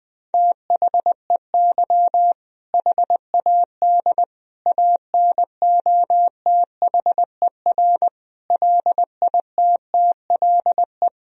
Morse code, heard as a keyed tone, about 20 wpm, 700 Hz. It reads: THEY HAD ANOTHER LITTLE
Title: Sentences from Top 1000 Words - Review Part 1 - 20wpm